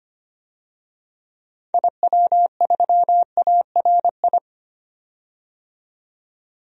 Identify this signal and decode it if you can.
Morse code, heard as a keyed tone, about 25 wpm, 700 Hz. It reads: IW3ARI
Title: DXCC-100 Call Signs - 6 Characters - 25wpm